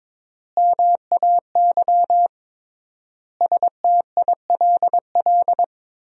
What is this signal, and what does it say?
Morse code, heard as a keyed tone, about 22 wpm, 700 Hz. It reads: MAY STILL
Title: 3x Word-Spacing - Sentences from Top 300 Words - 22wpm